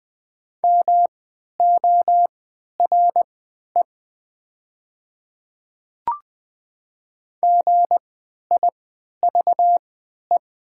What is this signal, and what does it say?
Morse code, heard as a keyed tone, about 20 wpm, spaced as for 11 wpm, 700 Hz. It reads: MORE GIVE
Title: 3x Character Spacing - Top 200 Words - 20wpm